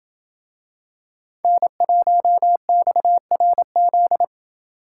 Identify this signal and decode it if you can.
Morse code, heard as a keyed tone, about 26 wpm, 700 Hz. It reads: N1XRZ